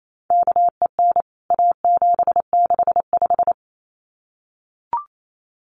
Morse code, heard as a keyed tone, about 28 wpm, 700 Hz. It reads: KEN A765
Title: CWT Exchange 28wpm